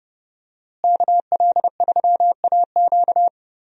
Morse code, heard as a keyed tone, about 32 wpm, 700 Hz. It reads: KL3AQ
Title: USA Call Signs - 5 Characters - Repeated 3x - 30wpm